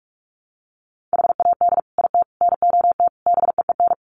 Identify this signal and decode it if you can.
Morse code, heard as a keyed tone, about 44 wpm, 700 Hz. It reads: HAD IT NOT BEEN
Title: Sets of 5 Words Encore2 45wpm